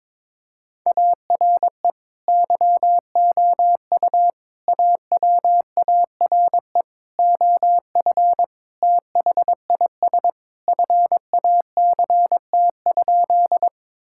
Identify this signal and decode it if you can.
Morse code, heard as a keyed tone, about 22 wpm, 700 Hz. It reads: ARE YOU AWARE OF THIS FACT?
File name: practice / 22wpm / Sentences with 2-Word Phrases - 22wpm